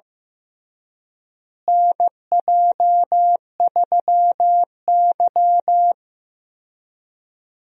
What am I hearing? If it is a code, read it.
Morse code, heard as a keyed tone, about 15 wpm, 700 Hz. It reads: NJ3Y
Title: USA Call Signs - 4 Characters - Repeated 3x - 15wpm